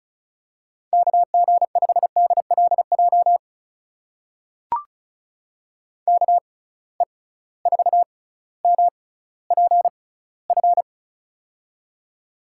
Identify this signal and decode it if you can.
Morse code, heard as a keyed tone, about 36 wpm, spaced as for 18 wpm, 700 Hz. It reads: KG5DLJ KE4MPF
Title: USA Call Signs - 6x Character Spacing - 35wpm